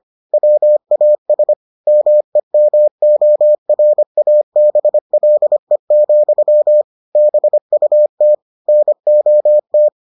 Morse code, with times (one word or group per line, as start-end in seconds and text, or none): 0.3-1.5 WAS
1.9-6.8 MEMORABLE,
7.2-8.3 BUT
8.7-9.9 NOT